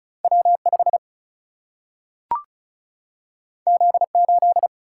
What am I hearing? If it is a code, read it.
Morse code, heard as a keyed tone, about 36 wpm, 700 Hz. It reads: W 5 Z 8
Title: Single-Letter Single-Number 35wpm